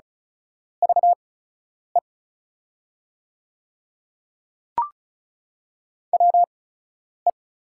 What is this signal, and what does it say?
Morse code, heard as a keyed tone, about 36 wpm, spaced as for 10 wpm, 690 Hz. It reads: VE WE
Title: 8x Character Spacing - Top 100 Words - 35wpm